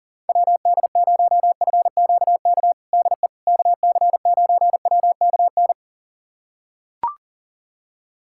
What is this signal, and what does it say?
Morse code, heard as a keyed tone, about 37 wpm, 700 Hz. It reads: WD0FQK DE KC9WKN
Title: USA Callsign DE Callsign 40wpm